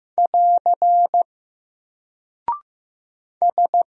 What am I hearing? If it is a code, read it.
Morse code, heard as a keyed tone, about 15 wpm, 700 Hz. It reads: <AR> S
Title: Single Character (Letter:Number:Punctuation:Prosign) Alt 15wpm